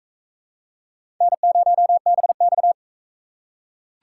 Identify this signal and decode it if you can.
Morse code, heard as a keyed tone, about 36 wpm, 700 Hz. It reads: N0BX